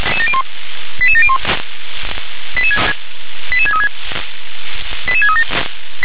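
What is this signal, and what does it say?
Five-tone selective call (ZVEI), heard as quick runs of five tones, tones of about 70 ms. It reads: 88701 88636 88637
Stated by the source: ZVEI five-tone selective call 88730 received on SDR.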